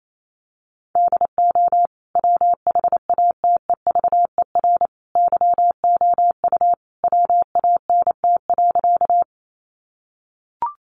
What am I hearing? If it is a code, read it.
Morse code, heard as a keyed tone, about 28 wpm, 700 Hz. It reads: DO WHATEVER YOU WANT.
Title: Sentences from Top 1100 Words 28wpm